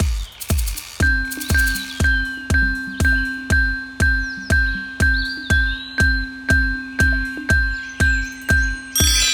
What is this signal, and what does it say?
There is no code.